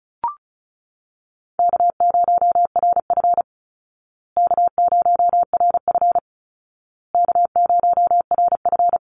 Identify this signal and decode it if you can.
Morse code, heard as a keyed tone, about 34 wpm, 700 Hz. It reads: K0RF K0RF K0RF
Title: Easy Call Signs Repeated 3x - 35wpm